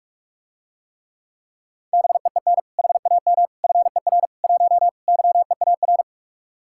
Morse code, heard as a keyed tone, about 49 wpm, 700 Hz. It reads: BEEN HAM FER 1 YEAR